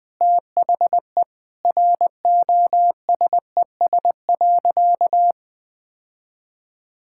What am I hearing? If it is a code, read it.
Morse code, heard as a keyed tone, about 20 wpm, 700 Hz. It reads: THE ROSES.
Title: Easy Sentences with Hints - 20wpm